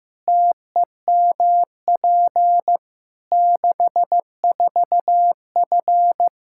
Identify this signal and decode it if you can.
Morse code, heard as a keyed tone, about 15 wpm, 700 Hz. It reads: TEMP 64F
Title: Practice QSO - Exchange 2 - Info - 15wpm